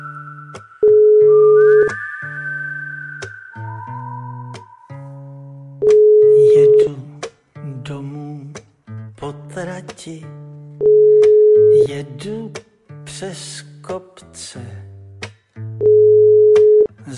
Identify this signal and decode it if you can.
Ringback tone (Europe, 425 Hz), with speech.
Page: Uvítací melodii uslyší volající, než přijmete hovor.